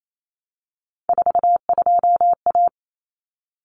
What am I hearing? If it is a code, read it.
Morse code, heard as a keyed tone, about 28 wpm, 700 Hz. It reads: 42A
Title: Common Cut Numbers - 3-Digit 28wpm